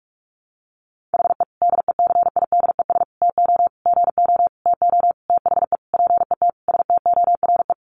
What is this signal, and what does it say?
Morse code, heard as a keyed tone, about 44 wpm, 700 Hz. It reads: HE DECIDES TO GO TO THE PET STORE
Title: Additional Easy Phrases 45wpm